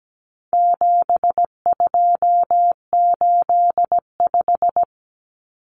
Morse code, heard as a keyed tone, about 17 wpm, 700 Hz. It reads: 7285
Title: Common Cut Numbers - 4-Digit 17wpm